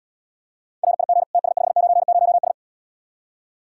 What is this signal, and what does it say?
Morse code, heard as a keyed tone, about 80 wpm, 700 Hz. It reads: RIG TS590S